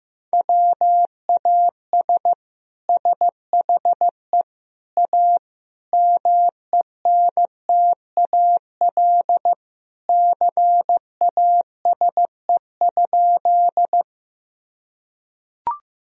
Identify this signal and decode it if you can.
Morse code, heard as a keyed tone, about 15 wpm, 700 Hz. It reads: WAS SHE A MENTAL CASE?
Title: Sentences from Top 1000 Words - Review Part 4 - 15wpm